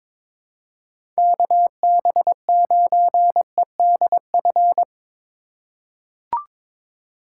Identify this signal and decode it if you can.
Morse code, heard as a keyed tone, about 22 wpm, 700 Hz. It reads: KB9EDF
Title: USA Call Signs - 6 Characters - 6x Character Spacing - 22wpm